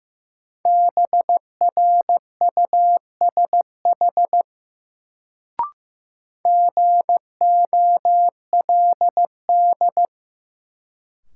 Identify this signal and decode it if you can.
Morse code, heard as a keyed tone, about 15 wpm, 700 Hz. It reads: BRUSH GOLD
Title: Sentences from Top 1100 Words - New Word Drill Alt - 15wpm